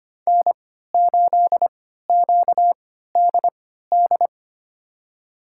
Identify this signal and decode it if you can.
Morse code, heard as a keyed tone, about 25 wpm, spaced as for 14 wpm, 700 Hz. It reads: N8QDD
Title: USA Call Signs - 5 Characters - 3x Character Spacing - 25wpm